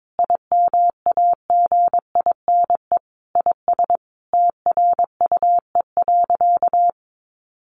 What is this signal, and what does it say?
Morse code, heard as a keyed tone, about 22 wpm, 700 Hz. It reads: IMAGINE IS TRUE.
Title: Sentences from Top 1400 Words 22wpm